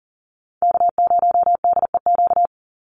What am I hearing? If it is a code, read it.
Morse code, heard as a keyed tone, about 40 wpm, 700 Hz. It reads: K0DEQ
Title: Call Signs Repeated 2x - 40wpm